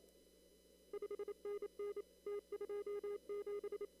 Morse code, heard as a keyed tone, about 27 wpm, 410 Hz. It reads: 5NN T27
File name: CW / kt5j wpx cw-op-k5pi